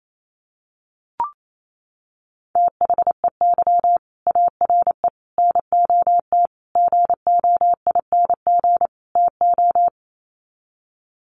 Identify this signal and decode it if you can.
Morse code, heard as a keyed tone, about 28 wpm, 700 Hz. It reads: THEY ARE NOT GOING TO
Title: Sets of 5 Words Encore2 28wpm